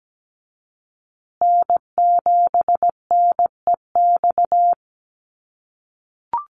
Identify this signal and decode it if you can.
Morse code, heard as a keyed tone, about 17 wpm, 700 Hz. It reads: N7NEX
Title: USA Call Signs Repeated 2x 17wpm